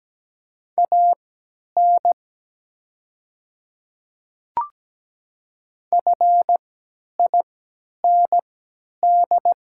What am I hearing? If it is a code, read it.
Morse code, heard as a keyed tone, about 17 wpm, spaced as for 10 wpm, 700 Hz. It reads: AN FIND
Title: 3x Character Spacing - Top 300 Words - 17wpm